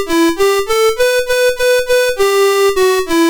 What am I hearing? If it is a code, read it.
There is no code.